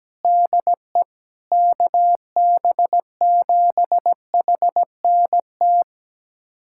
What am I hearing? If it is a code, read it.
Morse code, heard as a keyed tone, about 17 wpm, 700 Hz. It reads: DE KB7HNT
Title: USA Callsign DE Callsign - 6 Characters - 17wpm